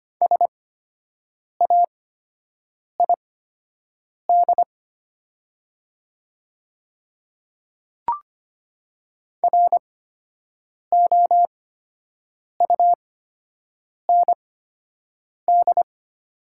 Morse code, heard as a keyed tone, about 25 wpm, spaced as for 7 wpm, 700 Hz. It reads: SAID ROUND
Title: 8x Character Spacing - Top 300 Words - 25wpm